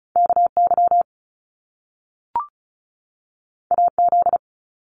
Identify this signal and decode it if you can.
Morse code, heard as a keyed tone, about 36 wpm, 700 Hz. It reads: KY AZ
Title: US States 35wpm